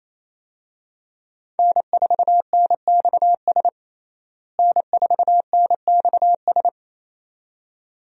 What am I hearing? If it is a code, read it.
Morse code, heard as a keyed tone, about 27 wpm, 700 Hz. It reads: N4NXS N4NXS
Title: USA Call Signs - 5 Characters - Repeated 2x - 28wpm